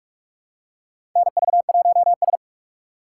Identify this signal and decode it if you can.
Morse code, heard as a keyed tone, about 46 wpm, 700 Hz. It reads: NV1S
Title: USA Call Signs - 4 Characters - 4x Character Spacing - 45wpm